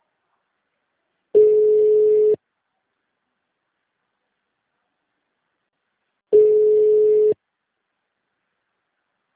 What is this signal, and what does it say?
Ringback tone (Europe, 425 Hz).